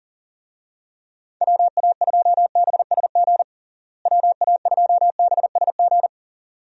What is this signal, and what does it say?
Morse code, heard as a keyed tone, about 37 wpm, 700 Hz. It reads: WA2BSG WA2BSG
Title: USA Call Signs - 6 Characters - Repeated 2x - 40wpm